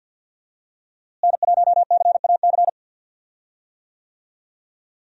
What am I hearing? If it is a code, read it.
Morse code, heard as a keyed tone, about 52 wpm, 700 Hz. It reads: N1CAC
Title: USA Call Signs - 5 Characters - Repeated 2x - 50wpm